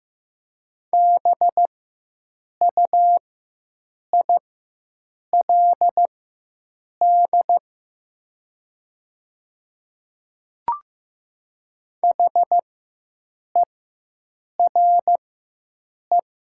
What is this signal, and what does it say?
Morse code, heard as a keyed tone, about 15 wpm, spaced as for 7 wpm, 700 Hz. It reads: BUILD HERE
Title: 4x Character Spacing - Top 200 Words - 15wpm